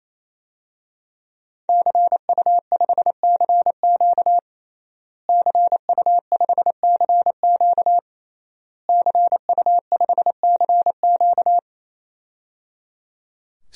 Morse code, heard as a keyed tone, about 28 wpm, 700 Hz. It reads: CU5CQ CU5CQ CU5CQ
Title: DXCC-100 Call Signs - 5 Characters - Repeated 3x - 28wpm